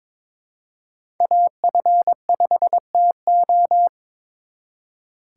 Morse code, heard as a keyed tone, about 26 wpm, 700 Hz. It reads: AF5TO
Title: Speed Racing - USA 2x2 Call Signs 22wpm